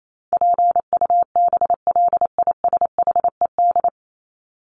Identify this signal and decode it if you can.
Morse code, heard as a keyed tone, about 28 wpm, 700 Hz. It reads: PUBLISHED
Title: ED Suffix - Top 100 Words with Base Words 28wpm